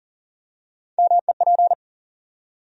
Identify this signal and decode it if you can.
Morse code, heard as a keyed tone, about 40 wpm, 700 Hz. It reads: MEP
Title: USA Call Sign Postfixes - 3 Letters 40wpm